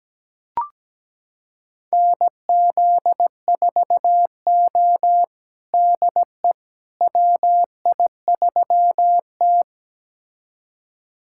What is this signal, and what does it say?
Morse code, heard as a keyed tone, about 17 wpm, 700 Hz. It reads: NZ4O DE WI3T
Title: USA Callsign DE Callsign - 4 Characters - 17wpm